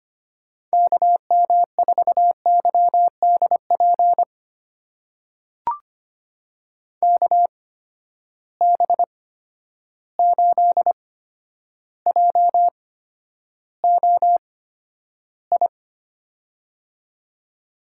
Morse code, heard as a keyed tone, about 25 wpm, spaced as for 11 wpm, 700 Hz. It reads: KM4YDP KB8JOI
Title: USA Call Signs - 6 Characters - 8x Character Spacing - 25wpm